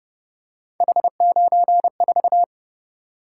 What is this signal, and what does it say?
Morse code, heard as a keyed tone, about 30 wpm, 700 Hz. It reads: H94